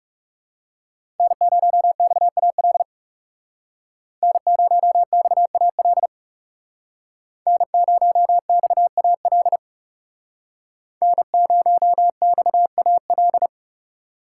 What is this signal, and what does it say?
Morse code, heard as a keyed tone, about 40 wpm, 700 Hz. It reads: N0XAL N0XAL N0XAL N0XAL